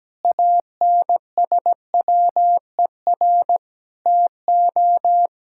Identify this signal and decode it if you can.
Morse code, heard as a keyed tone, about 17 wpm, 700 Hz. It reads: ANSWER TO
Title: Sentences from Top 1000 Words - Review Part 6 - 17wpm